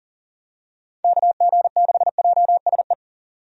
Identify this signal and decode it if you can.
Morse code, heard as a keyed tone, about 38 wpm, 700 Hz. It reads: KG6JSE
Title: USA Call Signs - 6 Characters - 40wpm